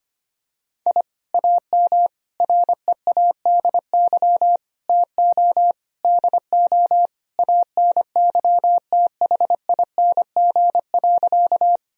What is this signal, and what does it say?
Morse code, heard as a keyed tone, about 25 wpm, 700 Hz. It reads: I AM READY TO DO ANYTHING.
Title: Sentences from Top 1000 Words - Review Part 2 - 25wpm